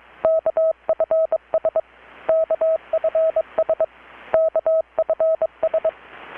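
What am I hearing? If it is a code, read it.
Morse code, heard as a keyed tone, about 21 wpm, 640 Hz. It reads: KFS KFS KFS
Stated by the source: Morse from KPH's Night of Nights XX maritime radiotelegraphy event, received in Chiapas, Mexico about 3,800 km away on a Kenwood R-600 with a grounded 10 m long-wire antenna.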